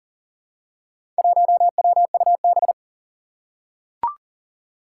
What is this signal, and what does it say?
Morse code, heard as a keyed tone, about 40 wpm, 700 Hz. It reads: 1WUB
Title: Four Challenging Letter-Number Combinations 40wpm